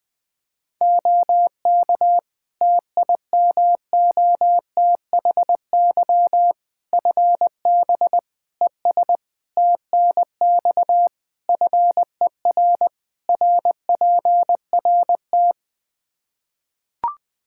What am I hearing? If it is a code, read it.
Morse code, heard as a keyed tone, about 20 wpm, 700 Hz. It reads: OK TIMOTHY FB ES TNX FER RPRT
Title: K1USN SST QSO Practice 20wpm